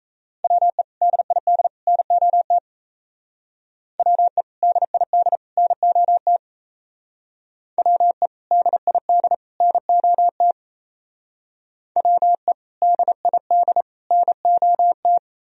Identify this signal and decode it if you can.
Morse code, heard as a keyed tone, about 34 wpm, 700 Hz. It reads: WE DID NOT WE DID NOT WE DID NOT WE DID NOT
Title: Speed-Racing - Sets of 3 Words 28wpm